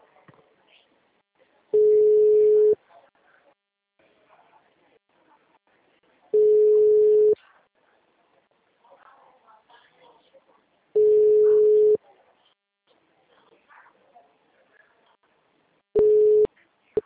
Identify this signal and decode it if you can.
Ringback tone (Europe, 425 Hz), with speech.